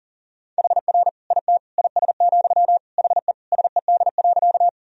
Morse code, heard as a keyed tone, about 37 wpm, 700 Hz. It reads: HR IT IS, HE SED.
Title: Sentences from Top 200 Words and Abbreviations 40wpm